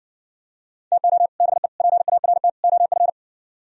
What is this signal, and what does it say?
Morse code, heard as a keyed tone, about 60 wpm, 700 Hz. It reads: TO BE PART OF